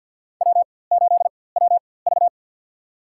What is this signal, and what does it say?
Morse code, heard as a keyed tone, about 51 wpm, spaced as for 23 wpm, 700 Hz. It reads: W8WV
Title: USA Call Signs - 4x Character Spacing - 50wpm